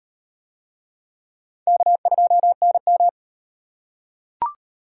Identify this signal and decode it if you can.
Morse code, heard as a keyed tone, about 35 wpm, 700 Hz. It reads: K2NM